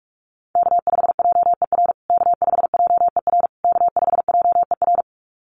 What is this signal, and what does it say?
Morse code, heard as a keyed tone, about 44 wpm, 690 Hz. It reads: K5JER K5JER K5JER
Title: USA Call Signs Repeated 3x 45wpm